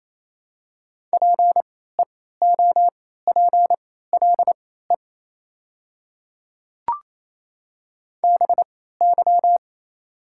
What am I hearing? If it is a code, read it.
Morse code, heard as a keyed tone, about 28 wpm, spaced as for 16 wpm, 700 Hz. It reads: PEOPLE BY
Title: 3x Character Spacing - Top 100 Words - 28wpm